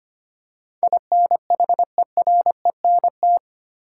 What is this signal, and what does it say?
Morse code, heard as a keyed tone, about 25 wpm, 700 Hz. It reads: INHERENT
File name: Eight Letter Words - 25wpm